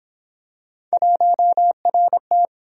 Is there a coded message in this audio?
Yes: Morse code from 0.9 to 2.5 s, heard as a keyed tone, about 29 wpm, 700 Hz. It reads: 1RT